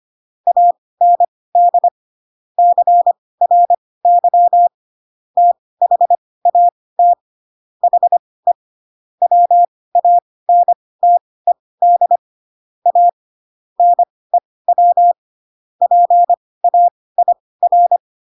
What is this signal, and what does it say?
Morse code, heard as a keyed tone, about 25 wpm, spaced as for 18 wpm, 700 Hz. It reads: AND CRY THAT HE WANTED A NEW PAIR